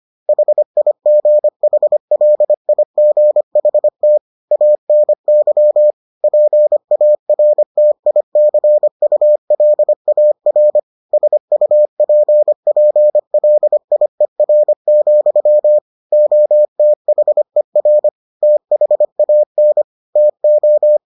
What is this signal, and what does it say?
Morse code, heard as a keyed tone, about 25 wpm, 600 Hz. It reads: HIGHLIGHT ANY PARTICULAR SUPPLIER, OTHER THAN TO